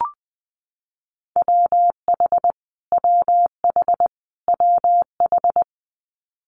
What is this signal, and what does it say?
Morse code, heard as a keyed tone, about 19 wpm, 700 Hz. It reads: WH WH WH
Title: Multi-letter Phonemes Repeated 3x 20wpm